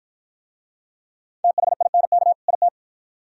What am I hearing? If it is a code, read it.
Morse code, heard as a keyed tone, about 48 wpm, 700 Hz. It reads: THINK IT